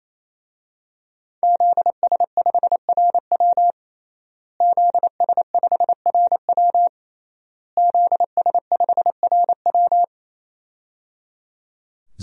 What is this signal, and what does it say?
Morse code, heard as a keyed tone, about 27 wpm, 700 Hz. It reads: ZS5RW ZS5RW ZS5RW
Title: DXCC-100 Call Signs - 5 Characters - Repeated 3x - 28wpm